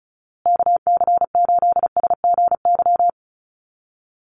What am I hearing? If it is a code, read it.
Morse code, heard as a keyed tone, about 34 wpm, 700 Hz. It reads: KC8SGY
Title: USA Call Signs Repeated 2x 35wpm